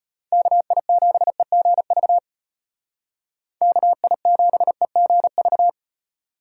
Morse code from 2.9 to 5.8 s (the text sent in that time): KI7EGV